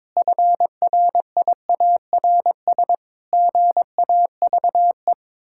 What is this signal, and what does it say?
Morse code, heard as a keyed tone, about 22 wpm, 700 Hz. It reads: FRIARS GAVE
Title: Pangram Challenge v2 - 22wpm